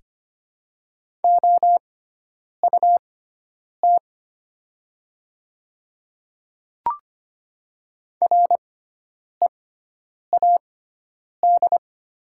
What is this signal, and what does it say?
Morse code, heard as a keyed tone, about 25 wpm, spaced as for 9 wpm, 700 Hz. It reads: OUT READ
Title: 6x Character Spacing - Top 200 Words - 25wpm